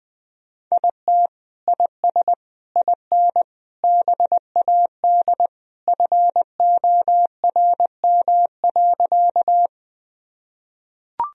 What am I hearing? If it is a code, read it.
Morse code, heard as a keyed tone, about 20 wpm, 700 Hz. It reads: IT IS IN BAD FORM.
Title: Sentences from Top 1000 Words - Review Part 1 - 20wpm